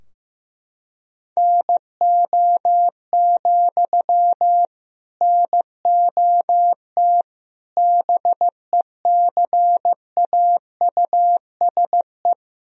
Morse code, heard as a keyed tone, about 15 wpm, 700 Hz. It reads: NO, NOT BECAUSE
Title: Sentences from Top 1000 Words - Review Part 3 - 15wpm